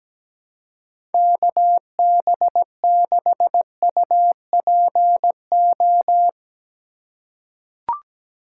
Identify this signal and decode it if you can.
Morse code, heard as a keyed tone, about 17 wpm, 700 Hz. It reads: KB6UPO
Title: USA Call Signs - 6 Characters - 2x Character Spacing - 17wpm